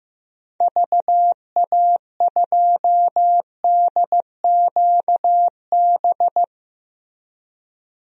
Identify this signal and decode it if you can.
Morse code, heard as a keyed tone, about 15 wpm, 700 Hz. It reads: VA2DQB
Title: DXCC-100 Call Signs - 6 Characters - Repeated 3x - 15wpm